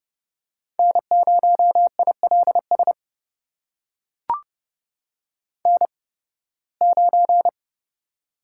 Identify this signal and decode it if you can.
Morse code, heard as a keyed tone, about 30 wpm, spaced as for 19 wpm, 700 Hz. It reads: N0ILS N9
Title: USA Call Signs - 5 Characters - 8x Character Spacing - 30wpm